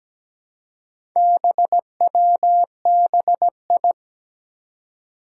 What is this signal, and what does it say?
Morse code, heard as a keyed tone, about 17 wpm, 700 Hz. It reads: BWBI
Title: Four Challenging Letter-Number Combinations 17wpm